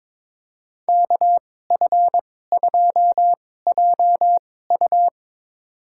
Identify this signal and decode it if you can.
Morse code, heard as a keyed tone, about 22 wpm, spaced as for 16 wpm, 700 Hz. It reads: KF2JU
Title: USA Call Signs - 5 Characters - 2x Character Spacing - 22wpm